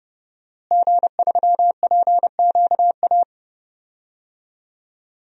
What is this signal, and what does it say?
Morse code, heard as a keyed tone, about 30 wpm, 700 Hz. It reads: G3PQA
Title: DXCC-100 Call Signs - 5 Characters - Repeated 2x - 30wpm